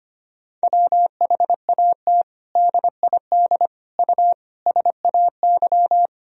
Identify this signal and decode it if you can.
Morse code, heard as a keyed tone, about 25 wpm, 700 Hz. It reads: WHAT DID U SAY
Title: Sentences from Top 200 Words and Abbreviations 25wpm